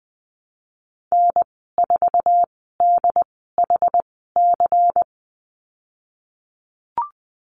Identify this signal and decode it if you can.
Morse code, heard as a keyed tone, about 20 wpm, spaced as for 15 wpm, 700 Hz. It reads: N4DHC
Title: 193 Lesson 31g - Letter v - Call Signs 20wpm